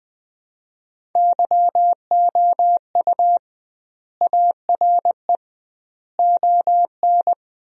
Morse code, heard as a keyed tone, about 20 wpm, 700 Hz. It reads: YOU ARE ON
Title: 2x Word-Spacing - Sentences from Top 400 Words - 20wpm